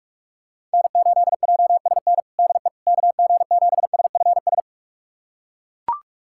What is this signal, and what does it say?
Morse code, heard as a keyed tone, about 41 wpm, 700 Hz. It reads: N8JSN DE KG7SFS
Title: USA Callsign DE Callsign 45wpm